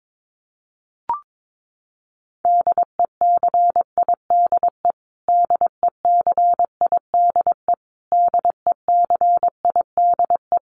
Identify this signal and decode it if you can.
Morse code, heard as a keyed tone, about 22 wpm, 700 Hz. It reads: DECIDE DECIDE DECIDE
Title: Top 500 Words Repeated 22wpm